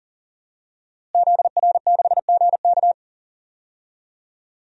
Morse code, heard as a keyed tone, about 40 wpm, 700 Hz. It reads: ZR6GK
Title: DXCC-100 Call Signs - 5 Characters - Repeated 3x - 40wpm